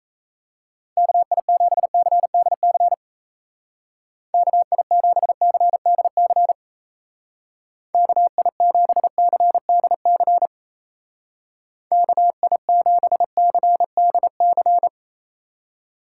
Morse code, heard as a keyed tone, about 34 wpm, 700 Hz. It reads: KI7CDC KI7CDC KI7CDC KI7CDC
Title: Speed Racing - USA 2x3 Call Signs 28wpm